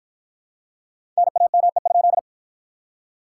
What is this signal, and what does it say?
Morse code, heard as a keyed tone, about 47 wpm, 700 Hz. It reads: NAME?